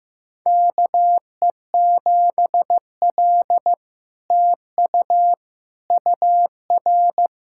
Morse code, heard as a keyed tone, about 15 wpm, 700 Hz. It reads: KE7L TU UR
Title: SOTA QSO Practice 15wpm